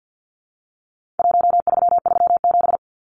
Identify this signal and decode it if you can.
Morse code, heard as a keyed tone, about 50 wpm, 700 Hz. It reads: 1337
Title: Numbers - 4-Digit 50wpm